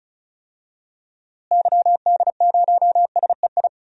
Morse code, heard as a keyed tone, about 34 wpm, 700 Hz. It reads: YD0SEI
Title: DXCC-100 Call Signs - 6 Characters - Repeated 2x - 35wpm